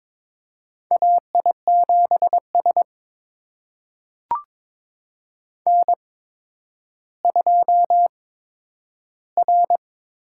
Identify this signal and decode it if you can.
Morse code, heard as a keyed tone, about 22 wpm, spaced as for 11 wpm, 700 Hz. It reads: AI7S N2R
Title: USA Call Signs - 8x Character Spacing - 22wpm